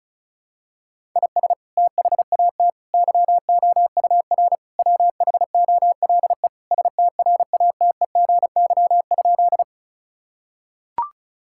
Morse code, heard as a keyed tone, about 36 wpm, 700 Hz. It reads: IS THAT YOUR WHOLE STRATEGY?
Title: Sentences from Top 1000 Words - Review Part 5 - 35wpm